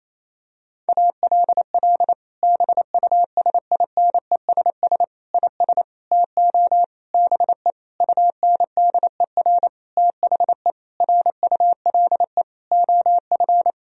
Morse code, heard as a keyed tone, about 28 wpm, 700 Hz. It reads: ALL BUSINESS IS TO BE UNDER THE RULE OF
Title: Sentences from Top 1000 Words - Review Part 3 - 28wpm